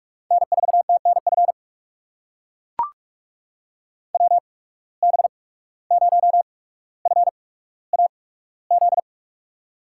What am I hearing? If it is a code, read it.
Morse code, heard as a keyed tone, about 44 wpm, spaced as for 18 wpm, 700 Hz. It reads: N4TNF WB0FAZ
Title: USA Call Signs - 8x Character Spacing - 45wpm